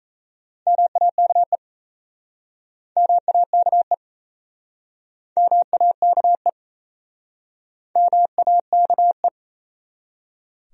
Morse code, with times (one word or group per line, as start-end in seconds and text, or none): 0.7-1.6 MAKE
3.0-3.9 MAKE
5.4-6.5 MAKE
8.0-9.3 MAKE